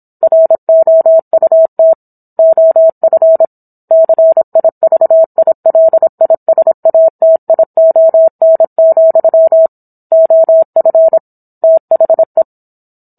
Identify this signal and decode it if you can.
Morse code, heard as a keyed tone, about 27 wpm, 650 Hz. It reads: ROUT OF CIVILISATION, OF THE
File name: Book One / 17-Chapter 17 - 26 WPM